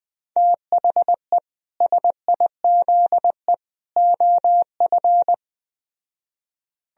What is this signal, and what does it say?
Morse code, heard as a keyed tone, about 20 wpm, 700 Hz. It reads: THE SIZE OF
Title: Speed-Racing - Sets of 3 Words 17wpm